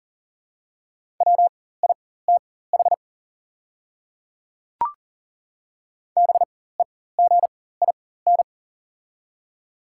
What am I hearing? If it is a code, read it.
Morse code, heard as a keyed tone, about 40 wpm, spaced as for 19 wpm, 700 Hz. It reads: WITH BEGIN
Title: 4x Character Spacing - Top 300 Words - 40wpm